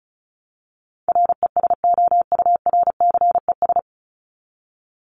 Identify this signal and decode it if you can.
Morse code, heard as a keyed tone, about 34 wpm, 700 Hz. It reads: RESOURCES
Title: ES:S Suffix - Top 100 Words with Base Words 35wpm